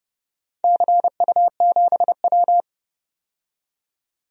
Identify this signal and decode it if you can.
Morse code, heard as a keyed tone, about 30 wpm, 700 Hz. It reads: CU7W
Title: DXCC-100 Call Signs - 4 Characters - 30wpm